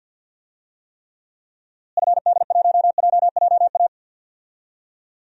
Morse code, heard as a keyed tone, about 49 wpm, 700 Hz. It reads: FD1JJA
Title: DXCC-100 Call Signs - 6 Characters - Repeated 3x - 50wpm